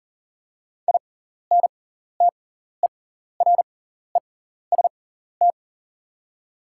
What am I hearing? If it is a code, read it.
Morse code, heard as a keyed tone, about 39 wpm, spaced as for 14 wpm, 700 Hz. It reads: INTEREST
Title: 6x Character Spacing - Top 500 Words - 40wpm